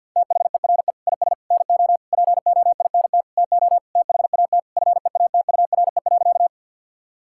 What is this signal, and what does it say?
Morse code, heard as a keyed tone, about 50 wpm, 700 Hz. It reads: THERE IS NO POINT TO THAT FEATURE.
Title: Sentences from Top 1000 Words - Review Part 5 - 50wpm